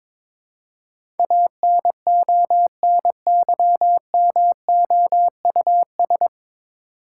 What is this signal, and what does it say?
Morse code, heard as a keyed tone, about 22 wpm, 700 Hz. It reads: ANONYMOUS
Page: Morse Code Ninja practice 22wpm